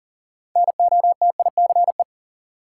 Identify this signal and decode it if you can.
Morse code, heard as a keyed tone, about 40 wpm, 700 Hz. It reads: NOTICE
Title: Sentences from Top 400 Words - New Word Drill - 40wpm